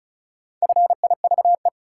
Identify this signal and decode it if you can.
Morse code, heard as a keyed tone, about 36 wpm, 700 Hz. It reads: FIVE
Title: Sentences from Top 400 Words - New Word Drill - 35wpm